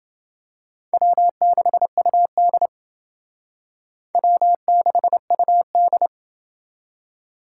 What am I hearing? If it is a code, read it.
Morse code, heard as a keyed tone, about 28 wpm, 700 Hz. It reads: W6UD W6UD